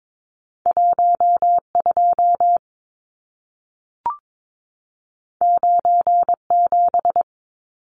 Morse code, heard as a keyed tone, about 22 wpm, 700 Hz. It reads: 12 97
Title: Numbers - 2-Digit 22wpm